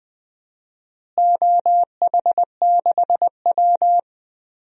Morse code, heard as a keyed tone, about 20 wpm, 700 Hz. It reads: OH6W
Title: DXCC-100 Call Signs - 4 Characters - Repeated 2x - 20wpm